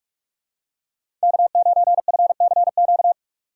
Morse code, heard as a keyed tone, about 45 wpm, 700 Hz. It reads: K9FCQ
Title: USA Call Signs - 5 Characters - 6x Character Spacing - 45wpm